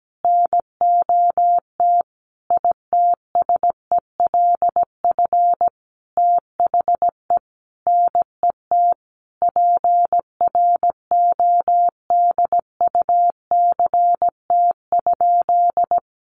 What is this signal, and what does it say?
Morse code, heard as a keyed tone, about 17 wpm, 700 Hz. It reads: NOT ITSELF THE NET PRODUCT?
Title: Sentences from Top 1400 Words 17wpm